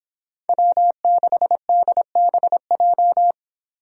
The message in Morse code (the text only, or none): W6DBJ